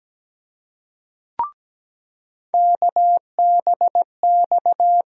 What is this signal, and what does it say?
Morse code, heard as a keyed tone, about 17 wpm, 700 Hz. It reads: KBX
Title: USA Call Sign Postfixes - 3 Letters 17wpm